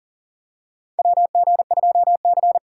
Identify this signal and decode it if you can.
Morse code, heard as a keyed tone, about 40 wpm, 700 Hz. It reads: WG2C
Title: USA Call Signs - 3x Character Spacing - 40wpm